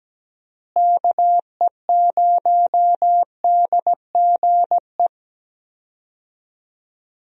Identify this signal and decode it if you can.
Morse code, heard as a keyed tone, about 17 wpm, 700 Hz. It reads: KE0DGE
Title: USA Call Signs - 6 Characters - Repeated 2x - 17wpm